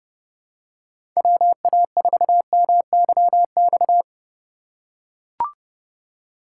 Morse code, heard as a keyed tone, about 32 wpm, 700 Hz. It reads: WA4MYX
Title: USA Call Signs - 3x Character Spacing - 30wpm